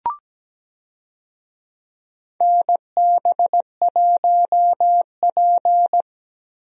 Morse code, heard as a keyed tone, about 17 wpm, 700 Hz. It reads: NB1P
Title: USA Call Signs - 4 Characters - Repeated 3x - 17wpm